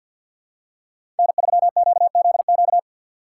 Morse code, heard as a keyed tone, about 48 wpm, 700 Hz. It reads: N3QZQ
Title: USA Call Signs - 4x Character Spacing - 50wpm